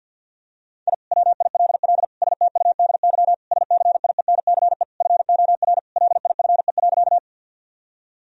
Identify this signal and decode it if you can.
Morse code, heard as a keyed tone, about 50 wpm, 700 Hz. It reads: I WILL STUDY SCIENCE FOR LIFE.
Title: Sentences from Top 1000 Words - Review Part 2 - 50wpm